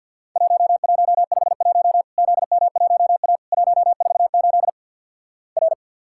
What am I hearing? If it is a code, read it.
Morse code, heard as a keyed tone, about 50 wpm, 700 Hz. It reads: 1151 ZM1A 148 R
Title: Worked All Europe DX Contest - QTC Exchange with Repeats - 50wpm